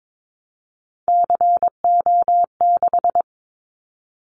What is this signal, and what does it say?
Morse code, heard as a keyed tone, about 22 wpm, 700 Hz. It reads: CO6
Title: Callsign Prefixes for Top 100 DXCC - 22wpm